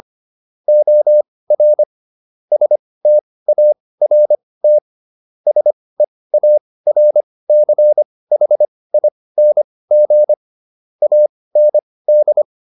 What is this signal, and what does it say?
Morse code, heard as a keyed tone, about 25 wpm, spaced as for 18 wpm, 600 Hz. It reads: OR START SEARCHING AND